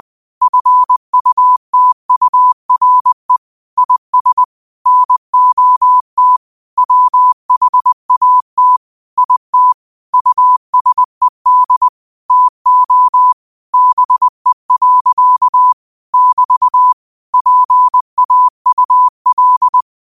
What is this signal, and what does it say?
Morse code, heard as a keyed tone, about 20 wpm, 1000 Hz. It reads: FUTURE IS NOT WHAT IT USED TO BE. <BT> PAUL
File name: 20wpm